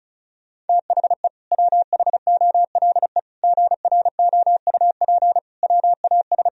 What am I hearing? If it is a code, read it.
Morse code, heard as a keyed tone, about 34 wpm, 700 Hz. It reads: THE WHOLE GROUP WAS
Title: Sentences from Top 1000 Words - Review Part 1 - 35wpm